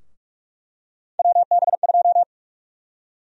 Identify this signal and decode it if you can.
Morse code, heard as a keyed tone, about 42 wpm, 700 Hz. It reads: WB2
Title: USA Call Sign Prefixes - Repeated 2x - 45wpm